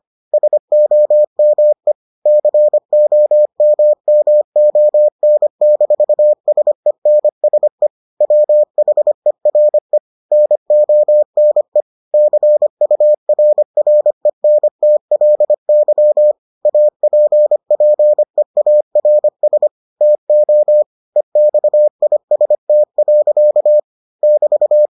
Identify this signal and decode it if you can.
Morse code, heard as a keyed tone, about 25 wpm, 600 Hz. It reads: SOME COMMON-SENSE WHERE NONE CURRENTLY APPEARS TO EXIST. <BT>